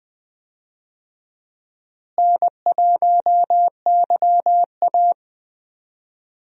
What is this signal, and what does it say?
Morse code, heard as a keyed tone, about 20 wpm, 700 Hz. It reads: N1YA
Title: USA Call Signs - 4 Characters - Repeated 3x - 20wpm